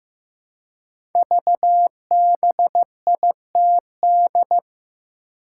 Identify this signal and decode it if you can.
Morse code, heard as a keyed tone, about 15 wpm, 700 Hz. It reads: VBITD
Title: Code Groups - 5 Letters - 15wpm